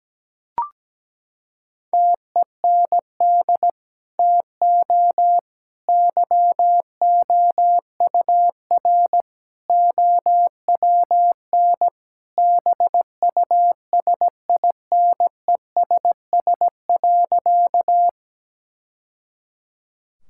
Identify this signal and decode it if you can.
Morse code, heard as a keyed tone, about 17 wpm, 700 Hz. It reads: TEND TO YOUR OWN BUSINESS.
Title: Sentences from Top 1000 Words - Review Part 5 - 17wpm